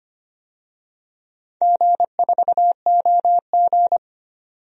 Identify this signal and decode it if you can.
Morse code, heard as a keyed tone, about 25 wpm, 700 Hz. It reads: G4OG
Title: DXCC-100 Call Signs - 4 Characters - 25wpm